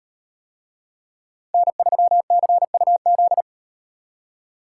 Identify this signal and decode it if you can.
Morse code, heard as a keyed tone, about 38 wpm, 700 Hz. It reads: N3CUZ